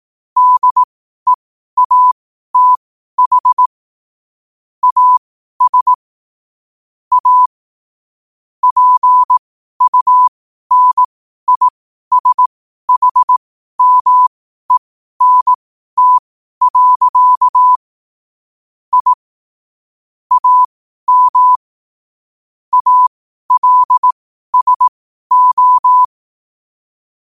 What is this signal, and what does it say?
Morse code, heard as a keyed tone, about 18 wpm, spaced as for 12 wpm, 1000 Hz. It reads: DEATH AS A PUNISHMENT. I AM ALSO
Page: New quotes every day in morse code at 12 Words per minute.